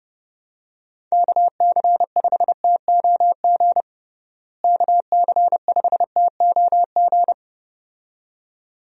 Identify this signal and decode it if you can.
Morse code, heard as a keyed tone, about 31 wpm, 700 Hz. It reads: KC5TOG KC5TOG
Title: USA Call Signs - 6 Characters - Repeated 2x - 30wpm